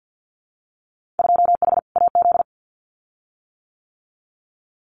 Morse code, heard as a keyed tone, about 50 wpm, 700 Hz. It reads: 2H AZ
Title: Winter Field Day Exchanges 50wpm